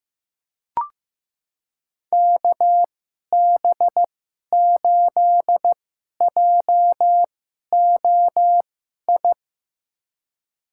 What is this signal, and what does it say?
Morse code, heard as a keyed tone, about 15 wpm, spaced as for 11 wpm, 700 Hz. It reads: KB8JOI
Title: USA Call Signs - 6 Characters - 2x Character Spacing - 15wpm